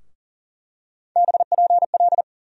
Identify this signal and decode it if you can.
Morse code, heard as a keyed tone, about 40 wpm, 700 Hz. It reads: BPL
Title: USA Call Sign Postfixes - 3 Letters 40wpm